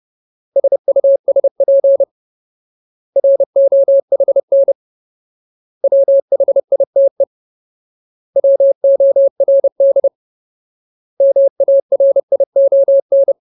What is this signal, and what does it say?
Morse code, heard as a keyed tone, about 30 wpm, 550 Hz. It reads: SUSP ROHN WHITE WORD MARION
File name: Ultimate Word List / 30WPM Ultimate